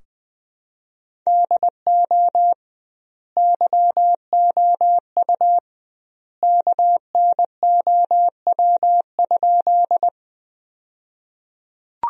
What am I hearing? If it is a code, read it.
Morse code, heard as a keyed tone, about 21 wpm, 700 Hz. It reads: DO YOU KNOW?
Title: 2x Word-Spacing - Sentences from Top 100 Words - 20wpm